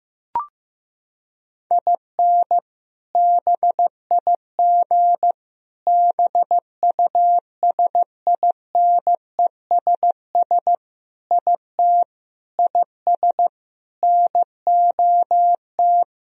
Morse code, heard as a keyed tone, about 15 wpm, 700 Hz. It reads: IN BIG BUSINESS IT IS NOT
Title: Sentences from Top 1000 Words - Review Part 1 - 15wpm